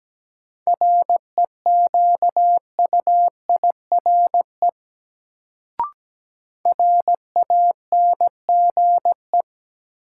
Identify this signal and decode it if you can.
Morse code, heard as a keyed tone, about 17 wpm, 700 Hz. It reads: REQUIRE RANGE
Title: Sentences from Top 900 Words - New Word Drill - 17wpm